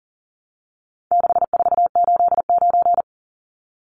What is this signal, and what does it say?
Morse code, heard as a keyed tone, about 40 wpm, 700 Hz. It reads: 6489
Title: Numbers - 4-Digit 40wpm